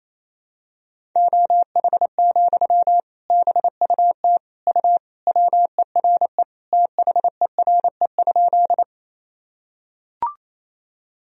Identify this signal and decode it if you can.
Morse code, heard as a keyed tone, about 28 wpm, 700 Hz. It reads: OH, BUT U WERE THERE?
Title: Sentences from Top 500 Words and Abbreviations 28wpm